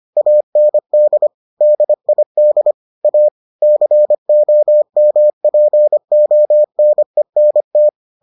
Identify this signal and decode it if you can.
Morse code, heard as a keyed tone, about 25 wpm, 600 Hz. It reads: AND DID A COMPONENT